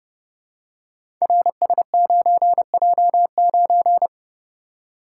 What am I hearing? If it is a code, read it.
Morse code, heard as a keyed tone, about 31 wpm, 700 Hz. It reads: RS9J9
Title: Code Groups - 5 Letter:Numbers - 30wpm